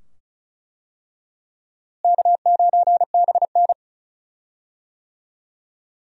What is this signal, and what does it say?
Morse code, heard as a keyed tone, about 34 wpm, 700 Hz. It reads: K9BN
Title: USA Call Signs - 4 Characters - Repeated 2x - 35wpm